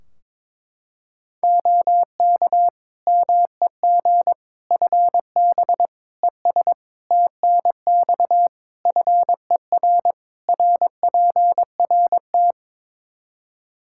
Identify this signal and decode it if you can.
Morse code, heard as a keyed tone, about 22 wpm, 700 Hz. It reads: OK MEG FB ES TNX FER RPRT
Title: Practice QSO - Exchange 2 - Info - 22wpm